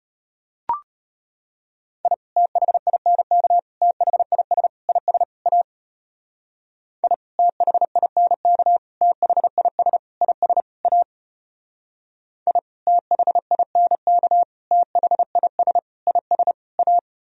Morse code, heard as a keyed tone, about 32 wpm, 700 Hz. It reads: I THINK THIS IS A I THINK THIS IS A I THINK THIS IS A